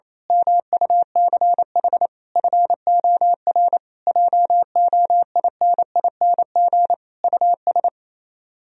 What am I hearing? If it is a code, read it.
Morse code, heard as a keyed tone, about 26 wpm, 700 Hz. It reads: MUCH FOR JOINING US
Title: Speed-Racing - Sets of 5 Words 28wpm